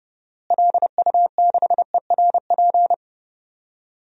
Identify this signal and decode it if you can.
Morse code, heard as a keyed tone, about 32 wpm, 700 Hz. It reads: LU6ERP
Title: DXCC-100 Call Signs - 6 Characters - Repeated 2x - 30wpm